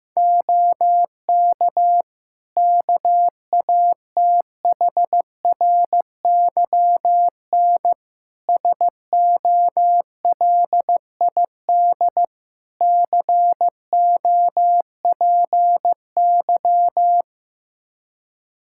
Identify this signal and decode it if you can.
Morse code, heard as a keyed tone, about 15 wpm, 700 Hz. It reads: OK KATHRYN SOLID COPY
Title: K1USN SST QSO Practice 15wpm